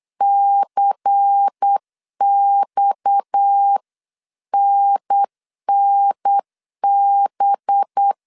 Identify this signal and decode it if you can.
Morse code, heard as a keyed tone, about 8 wpm, 800 Hz. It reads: CXNNB
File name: 8wpm